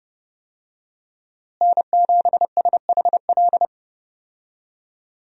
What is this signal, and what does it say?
Morse code, heard as a keyed tone, about 30 wpm, 700 Hz. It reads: N7SHL
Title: USA Call Signs - 5 Characters - Repeated 2x - 30wpm